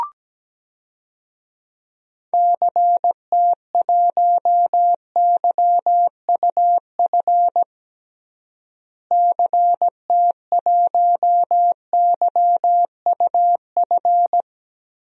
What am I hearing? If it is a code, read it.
Morse code, heard as a keyed tone, about 17 wpm, 700 Hz. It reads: CT1YUF CT1YUF
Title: DXCC-100 Call Signs - 6 Characters - Repeated 3x - 17wpm